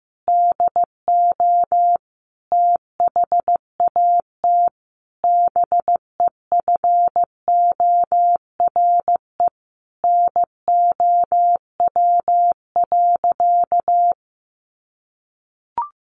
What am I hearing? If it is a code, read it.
Morse code, heard as a keyed tone, about 15 wpm, 700 Hz. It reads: DO THAT BEFORE NOW.
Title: Sentences from Top 1300 Words 15wpm